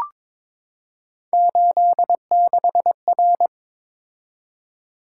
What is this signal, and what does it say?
Morse code, heard as a keyed tone, about 23 wpm, 700 Hz. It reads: 86R
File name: practice / 22wpm / Three Letter-Number Combinations - 22wpm